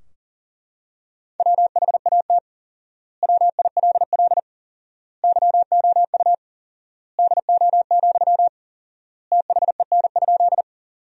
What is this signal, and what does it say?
Morse code, heard as a keyed tone, about 40 wpm, 700 Hz. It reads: WHAT WILL YOU DO, THEN?
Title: 4x Word-Spacing - Sentences from Top 100 Words - 40wpm